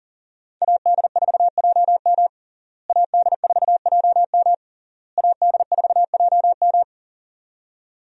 Morse code, heard as a keyed tone, about 40 wpm, 700 Hz. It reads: AD4JM AD4JM AD4JM
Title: USA Call Signs - 5 Characters - Repeated 3x - 40wpm